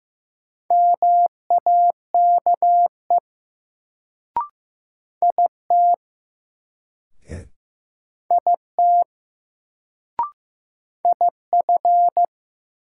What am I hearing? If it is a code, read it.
Morse code, heard as a keyed tone, about 15 wpm, 700 Hz. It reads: MAKE IT IT IF
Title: Sentences from Top 100 Words - New Word Drill - 15wpm